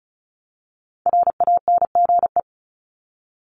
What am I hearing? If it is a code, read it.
Morse code, heard as a keyed tone, about 34 wpm, 700 Hz. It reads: RANGE
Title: Top 501-750 Words 35wpm